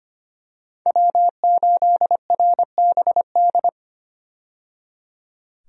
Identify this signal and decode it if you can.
Morse code, heard as a keyed tone, about 25 wpm, 690 Hz. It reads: W8RBD
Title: USA Call Signs - 5 Characters - 25wpm